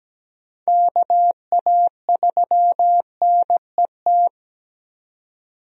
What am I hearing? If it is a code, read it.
Morse code, heard as a keyed tone, about 17 wpm, 700 Hz. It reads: KA3NET
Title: USA Call Signs - 6 Characters - 17wpm